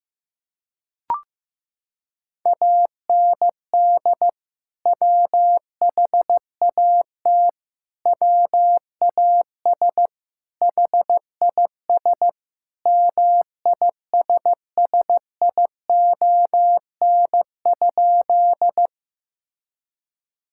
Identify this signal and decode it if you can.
Morse code, heard as a keyed tone, about 15 wpm, 700 Hz. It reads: AND WHAT WAS HIS MISSION?
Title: Sentences from Top 1600 Words 15wpm